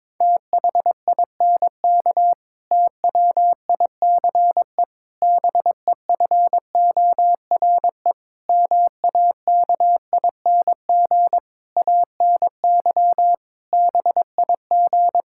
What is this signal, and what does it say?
Morse code, heard as a keyed tone, about 22 wpm, 700 Hz. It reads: THINK TWICE BEFORE MAKING ANY BIG
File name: Difficult Sentences with Hints - 22wpm